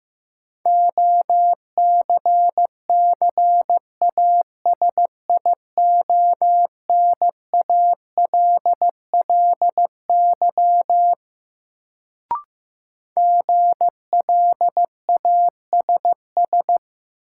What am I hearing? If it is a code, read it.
Morse code, heard as a keyed tone, about 15 wpm, 700 Hz. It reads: OCCASIONALLY GLASS
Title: Sentences from Top 1300 Words - New Word Drill Alt - 15wpm